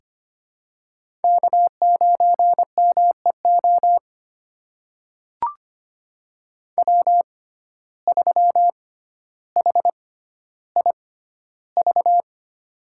Morse code, heard as a keyed tone, about 25 wpm, spaced as for 13 wpm, 700 Hz. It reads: K9MEO W3HIV
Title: USA Call Signs - 5 Characters - 6x Character Spacing - 25wpm